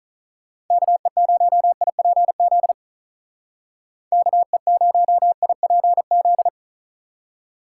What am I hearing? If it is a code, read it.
Morse code, heard as a keyed tone, about 44 wpm, 700 Hz. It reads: KE0IPZ KE0IPZ